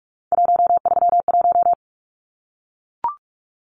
Morse code, heard as a keyed tone, about 44 wpm, 700 Hz. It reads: 131
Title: Numbers - 3-Digit 45wpm